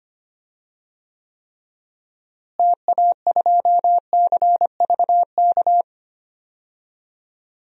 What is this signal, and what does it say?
Morse code, heard as a keyed tone, about 25 wpm, 700 Hz. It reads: TA2CVK
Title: DXCC-100 Call Signs - 6 Characters - Repeated 3x - 25wpm